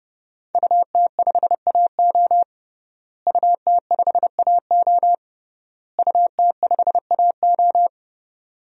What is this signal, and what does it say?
Morse code, heard as a keyed tone, about 30 wpm, 700 Hz. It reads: UT5AO UT5AO UT5AO
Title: DXCC-100 Call Signs - 5 Characters - Repeated 3x - 30wpm